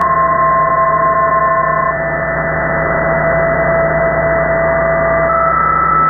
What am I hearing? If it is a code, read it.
PSK31, about 31.25 baud.
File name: Radio / SDR Audio / gqrx